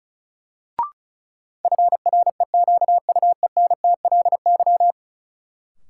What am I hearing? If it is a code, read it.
Morse code, heard as a keyed tone, about 34 wpm, 700 Hz. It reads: FREQUENTLY
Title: Sentences from Top 1300 Words - New Word Drill Alt - 35wpm